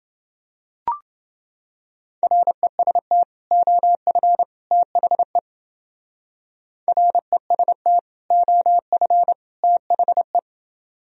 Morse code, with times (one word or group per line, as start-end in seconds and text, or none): 2.2-3.2 REST
3.5-4.4 OF
4.7-5.4 THE
6.9-8.0 REST
8.3-9.3 OF
9.6-10.4 THE